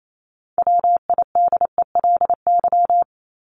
Morse code, heard as a keyed tone, about 28 wpm, 700 Hz. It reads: WIDELY
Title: LY Suffix - Top 100 Words with Base Words 28wpm